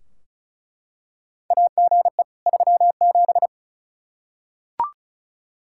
Morse code, heard as a keyed tone, about 34 wpm, 700 Hz. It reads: AGE 37
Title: Practice QSO - Exchange 3 - More Info - 35wpm